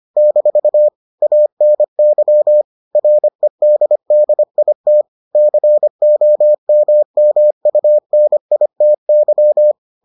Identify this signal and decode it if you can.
Morse code, heard as a keyed tone, about 25 wpm, 600 Hz. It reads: - ANY REDDIT COMMUNITY